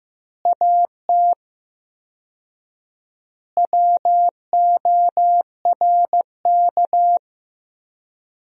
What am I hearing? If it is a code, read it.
Morse code, heard as a keyed tone, about 15 wpm, 700 Hz. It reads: AT WORK
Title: 4x Word-Spacing - Sentences from Top 200 Words - 15wpm